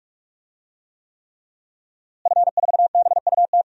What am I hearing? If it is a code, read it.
Morse code, heard as a keyed tone, about 41 wpm, 700 Hz. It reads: F4BUT
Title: DXCC-100 Call Signs - 5 Characters - Repeated 3x - 45wpm